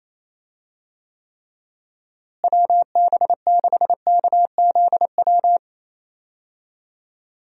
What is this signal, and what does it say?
Morse code, heard as a keyed tone, about 28 wpm, 700 Hz. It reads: WB6KZW
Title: USA Call Signs - 6 Characters - Repeated 3x - 28wpm